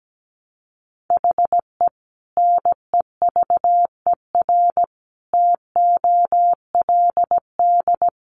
Morse code, heard as a keyed tone, about 17 wpm, 700 Hz. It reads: HE NEVER TOLD
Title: Sentences from Top 400 Words 17wpm